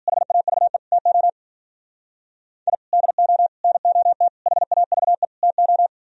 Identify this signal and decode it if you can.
Morse code, heard as a keyed tone, about 51 wpm, 700 Hz. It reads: HAVE TO I DO NOT HAVE TO